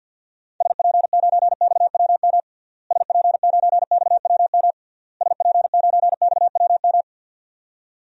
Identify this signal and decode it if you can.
Morse code, heard as a keyed tone, about 48 wpm, 700 Hz. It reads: SP9XWM SP9XWM SP9XWM
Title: DXCC-100 Call Signs - 6 Characters - Repeated 3x - 50wpm